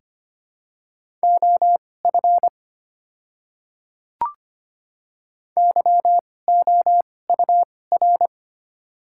Morse code, heard as a keyed tone, about 25 wpm, spaced as for 18 wpm, 700 Hz. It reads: OF YOUR
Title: 2x Character Spacing - Top 100 Words - 25wpm